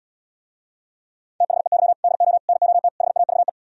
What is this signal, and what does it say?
Morse code, heard as a keyed tone, about 77 wpm, 700 Hz. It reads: THEY DO NOT HAVE